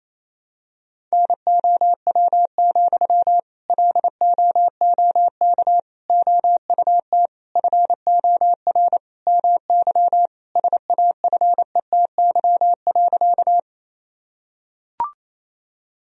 Morse code, heard as a keyed tone, about 28 wpm, 700 Hz. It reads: NOW, LOOK OUT FOR MY SAFETY.
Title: Sentences from Top 1000 Words - Review Part 4 - 28wpm